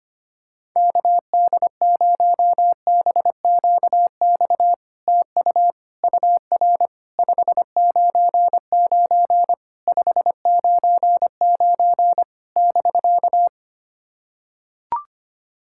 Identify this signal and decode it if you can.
Morse code, heard as a keyed tone, about 25 wpm, 700 Hz. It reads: KD0BQX TU UR 599 599 <BK>
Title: SOTA QSO Practice 25wpm